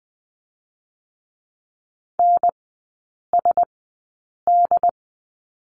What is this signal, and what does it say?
Morse code, heard as a keyed tone, about 19 wpm, 700 Hz. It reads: N S D
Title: 114 Lesson 19e - Letter w - Three Characters 20wpm